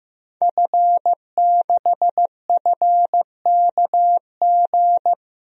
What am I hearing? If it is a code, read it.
Morse code, heard as a keyed tone, about 15 wpm, 700 Hz. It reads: F6FKG
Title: DXCC-100 Call Signs - 5 Characters - 15wpm